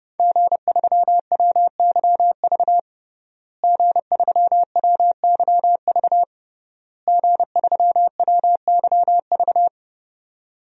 Morse code, heard as a keyed tone, about 30 wpm, 700 Hz. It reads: G3WYV G3WYV G3WYV
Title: DXCC-100 Call Signs - 5 Characters - Repeated 3x - 30wpm